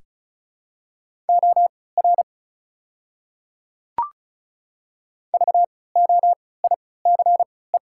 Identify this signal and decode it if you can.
Morse code, heard as a keyed tone, about 34 wpm, spaced as for 20 wpm, 700 Hz. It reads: OR VOICE
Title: 3x Character Spacing - Top 500 Words - 35wpm